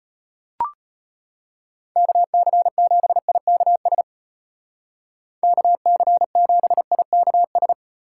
Morse code, heard as a keyed tone, about 35 wpm, 700 Hz. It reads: KC7IKS KC7IKS